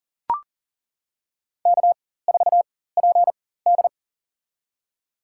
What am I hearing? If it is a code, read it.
Morse code, heard as a keyed tone, about 40 wpm, spaced as for 19 wpm, 700 Hz. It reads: K4PD
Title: USA Call Signs - 4 Characters - 4x Character Spacing - 40wpm